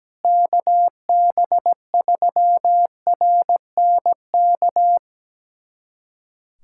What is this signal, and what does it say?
Morse code, heard as a keyed tone, about 17 wpm, 700 Hz. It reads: KB3RNK
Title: USA Callsign DE Callsign 17wpm